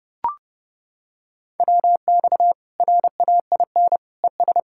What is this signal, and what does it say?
Morse code, heard as a keyed tone, about 30 wpm, 700 Hz. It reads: WX RAIN ES
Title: Practice QSO - Exchange 2 - Info - 30wpm